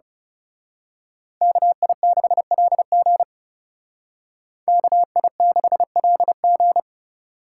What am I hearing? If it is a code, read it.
Morse code, heard as a keyed tone, about 31 wpm, 700 Hz. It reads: KI6LG KI6LG
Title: Speed Racing - USA 2x2 Call Signs 30wpm